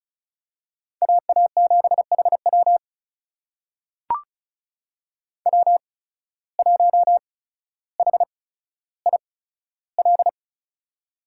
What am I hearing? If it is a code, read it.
Morse code, heard as a keyed tone, about 34 wpm, spaced as for 15 wpm, 690 Hz. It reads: AA7HW W1HIL
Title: USA Call Signs - 5 Characters - 8x Character Spacing - 35wpm